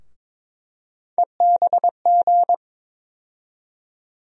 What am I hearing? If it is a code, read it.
Morse code, heard as a keyed tone, about 21 wpm, 700 Hz. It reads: EBG
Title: USA Call Sign Postfixes - 3 Letters 22wpm